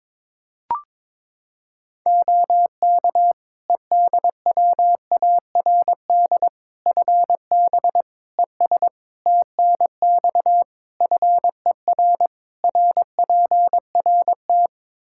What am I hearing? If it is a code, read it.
Morse code, heard as a keyed tone, about 22 wpm, 700 Hz. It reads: OK EDWARD FB ES TNX FER RPRT
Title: Practice QSO - Exchange 2 - Info - 22wpm